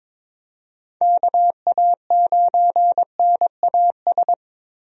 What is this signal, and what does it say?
Morse code, heard as a keyed tone, about 22 wpm, 700 Hz. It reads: KA9NAS
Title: USA Call Signs - 6 Characters - Repeated 2x - 22wpm